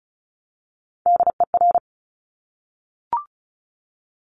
Morse code, heard as a keyed tone, about 36 wpm, 700 Hz. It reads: DER
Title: Top Three Letter Combinations in Words 35wpm